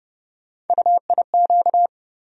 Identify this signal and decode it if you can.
Morse code, heard as a keyed tone, about 30 wpm, 700 Hz. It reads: UIQ
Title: USA Call Sign Postfixes - 3 Letters 30wpm